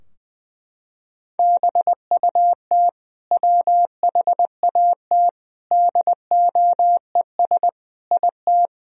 Morse code, heard as a keyed tone, about 21 wpm, 700 Hz. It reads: BUT WHAT DOES IT
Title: Sentences from Top 1000 Words - Review Part 4 - 20wpm